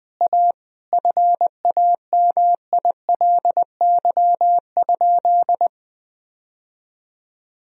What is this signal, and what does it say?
Morse code, heard as a keyed tone, about 20 wpm, 700 Hz. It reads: A FAMILY?
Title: Sentences from Top 1000 Words - Review Part 1 - 20wpm